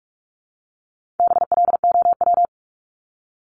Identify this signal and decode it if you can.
Morse code, heard as a keyed tone, about 44 wpm, 700 Hz. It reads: BLOW
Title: Top 1751-2000 Words 45wpm